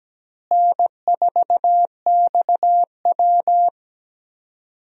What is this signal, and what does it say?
Morse code, heard as a keyed tone, about 17 wpm, 700 Hz. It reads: N4XW
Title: USA Call Signs - 4 Characters - Repeated 2x - 17wpm